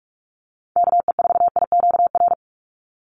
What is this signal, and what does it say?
Morse code, heard as a keyed tone, about 44 wpm, 700 Hz. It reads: KE4IQR
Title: USA Call Signs 45wpm